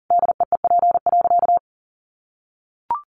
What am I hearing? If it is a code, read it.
Morse code, heard as a keyed tone, about 40 wpm, 700 Hz. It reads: DEEP.
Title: Sentences from Top 1100 Words 40wpm